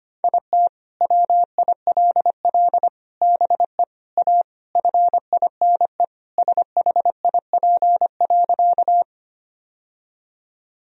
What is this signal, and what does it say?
Morse code, heard as a keyed tone, about 25 wpm, 700 Hz. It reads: IT WILL BE A FINE SHIP.
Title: Sentences from Top 1000 Words - Review Part 2 - 25wpm